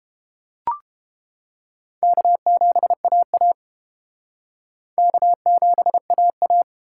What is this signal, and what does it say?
Morse code, heard as a keyed tone, about 30 wpm, 700 Hz. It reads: K7AA K7AA